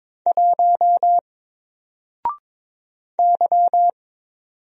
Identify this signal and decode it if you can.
Morse code, heard as a keyed tone, about 22 wpm, 700 Hz. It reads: 1 Y
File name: Single-Letter-Number-22wpm